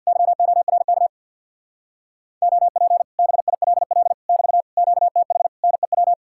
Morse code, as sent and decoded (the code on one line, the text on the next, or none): -.-- --- .-. -.- / --- .--. / -... .. .-.. .-.. / -...- / --.- - .... / -. .--
YORK OP BILL <BT> QTH NEW